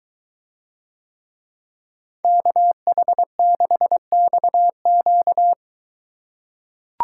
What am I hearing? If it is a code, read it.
Morse code, heard as a keyed tone, about 21 wpm, 700 Hz. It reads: KH6XQ